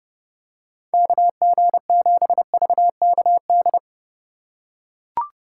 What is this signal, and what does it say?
Morse code, heard as a keyed tone, about 32 wpm, 700 Hz. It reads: KG7VKD
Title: USA Call Signs - 6 Characters - 4x Character Spacing - 30wpm